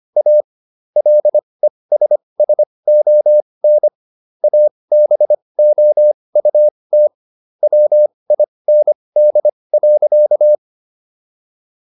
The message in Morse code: A LESSON ABOUT WIND.

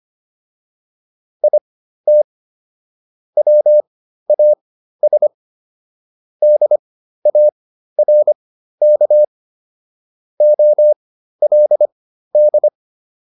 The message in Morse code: IT WAS DARK OLD

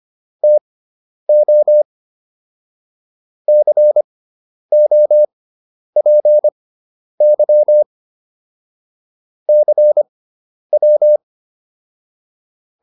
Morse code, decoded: TO COPY CW

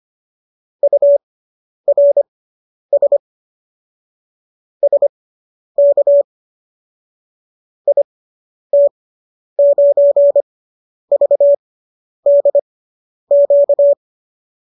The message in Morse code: URS SK IT9VDQ

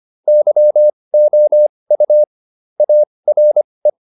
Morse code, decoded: YOU ARE

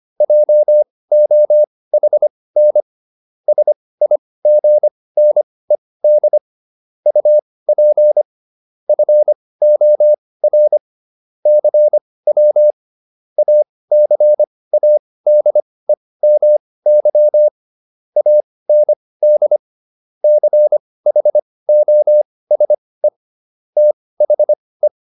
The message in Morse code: JOHN SIGNED UP FOR CW ACADEMY AND CHOSE THE

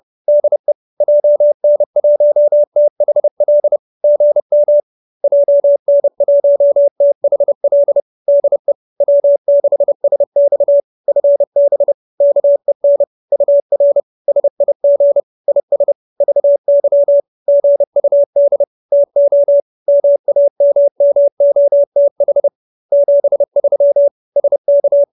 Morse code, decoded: DE JN1THL GM JN1THL DE W6SX FB KEN UR SIG IS VY GUD TO MAMMOTH 73 SK

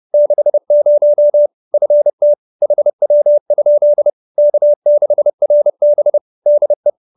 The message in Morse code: 60 FT HW? K6RB DE

